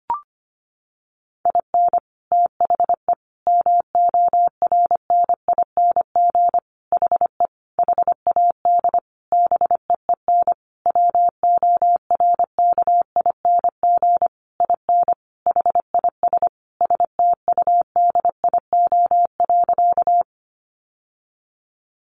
Morse code, heard as a keyed tone, about 25 wpm, 700 Hz. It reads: IN THE MORNING HE HAD BEEN WORKING IN HIS STUDIO.
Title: Sentences from Top 1200 Words 25wpm